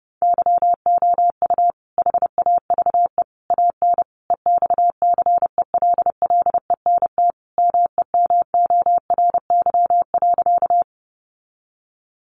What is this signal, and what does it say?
Morse code, heard as a keyed tone, about 30 wpm, 700 Hz. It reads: YOU HAVE AN EXCELLENT MEMORY.
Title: Sentences from Top 1300 Words 30wpm